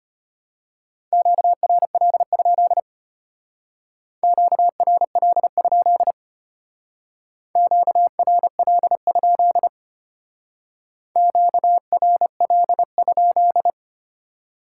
Morse code, heard as a keyed tone, about 32 wpm, 700 Hz. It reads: QRL? QRL? QRL? QRL?